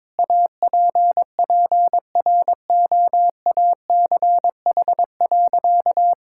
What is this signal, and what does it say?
Morse code, heard as a keyed tone, about 22 wpm, 700 Hz. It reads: APPROACH.
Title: Sentences from Top 1000 Words - Review Part 6 - 22wpm